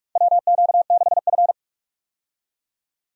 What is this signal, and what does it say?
Morse code, heard as a keyed tone, about 44 wpm, 700 Hz. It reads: WQ6F
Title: USA Call Signs - 4 Characters - 45wpm